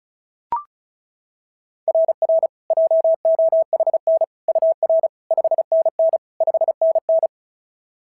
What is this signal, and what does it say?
Morse code, heard as a keyed tone, about 36 wpm, 690 Hz. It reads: RR JOHN UR 5NN 5NN
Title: SOTA QSO Practice 35wpm